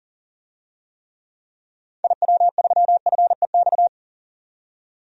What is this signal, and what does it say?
Morse code, heard as a keyed tone, about 40 wpm, 700 Hz. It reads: IW3FEX
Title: DXCC-100 Call Signs - 6 Characters - 40wpm